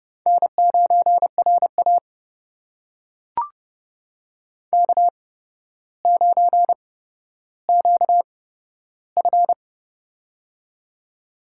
Morse code, heard as a keyed tone, about 30 wpm, spaced as for 13 wpm, 700 Hz. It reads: N9RA K9QF
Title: USA Call Signs - 4 Characters - 8x Character Spacing - 30wpm